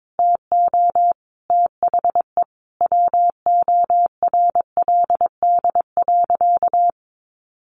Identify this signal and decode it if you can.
Morse code, heard as a keyed tone, about 22 wpm, 700 Hz. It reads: TO THE WORLD.
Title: Sentences from Top 1200 Words 22wpm